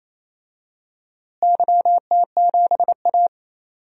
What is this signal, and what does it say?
Morse code, heard as a keyed tone, about 28 wpm, 700 Hz. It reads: YT7A
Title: DXCC-100 Call Signs - 4 Characters - Repeated 3x - 28wpm